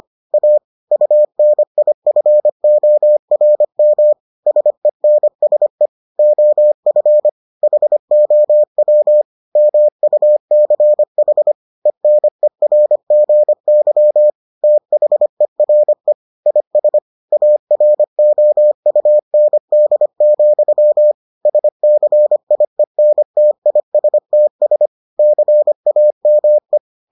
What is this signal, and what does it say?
Morse code, heard as a keyed tone, about 25 wpm, 600 Hz. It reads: A UNIFORM SENSE OF HOW MUCH ENERGY THERE IS AROUND, SCIENTISTS CAME